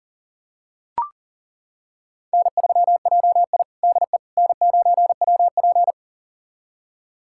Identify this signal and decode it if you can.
Morse code, heard as a keyed tone, about 38 wpm, 700 Hz. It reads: N3JI DE N9WP
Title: USA Callsign DE Callsign - 4 Characters - 40wpm